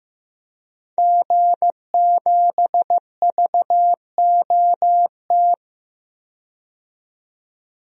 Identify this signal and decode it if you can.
Morse code, heard as a keyed tone, about 15 wpm, 700 Hz. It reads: G7VOT
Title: DXCC-100 Call Signs - 5 Characters - 15wpm